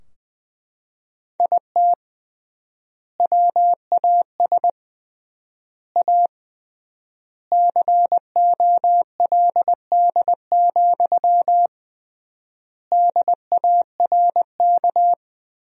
Morse code, heard as a keyed tone, about 20 wpm, 700 Hz. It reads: IT WAS A COLD, DARK
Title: 3x Word-Spacing - Sentences from Top 500 Words - 20wpm